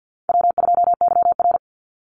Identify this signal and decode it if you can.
Morse code, heard as a keyed tone, about 48 wpm, 700 Hz. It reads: W2YR
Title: Easy Call Signs Repeated 2x - 50wpm